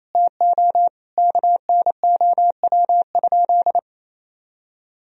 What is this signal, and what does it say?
Morse code, heard as a keyed tone, about 28 wpm, 700 Hz. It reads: TO KNOW?
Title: Sentences from Top 200 Words and Abbreviations 28wpm